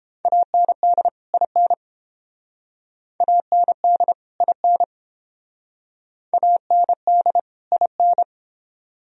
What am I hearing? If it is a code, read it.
Morse code, heard as a keyed tone, about 29 wpm, 700 Hz. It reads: AND IN AND IN AND IN